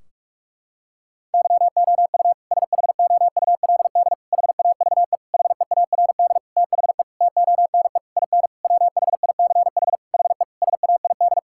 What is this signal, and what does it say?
Morse code, heard as a keyed tone, about 46 wpm, 700 Hz. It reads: YOU SHOULD HAVE HEARD THE TONE IN WHICH HE SAID
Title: Sentences from Top 1600 Words 45wpm